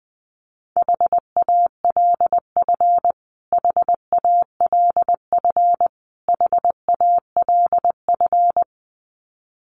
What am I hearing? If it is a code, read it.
Morse code, heard as a keyed tone, about 20 wpm, 700 Hz. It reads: HALF HALF HALF
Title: Top 500 Words Repeated 20wpm